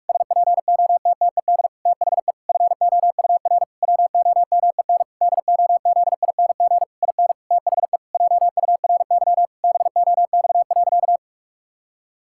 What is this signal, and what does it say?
Morse code, heard as a keyed tone, about 46 wpm, 700 Hz. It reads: SPOTTED THE FOUR WOMEN DOZING IN THE JURY BOX.
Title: Pangram Challenge v2 - 45wpm